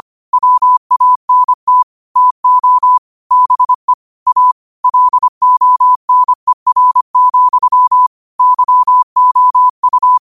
Morse code, heard as a keyed tone, about 25 wpm, 1000 Hz. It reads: WANT TO BE A LONER, YOU